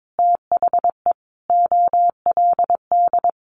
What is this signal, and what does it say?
Morse code, heard as a keyed tone, about 22 wpm, 700 Hz. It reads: THE OLD
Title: Sentences from Top 900 Words 22wpm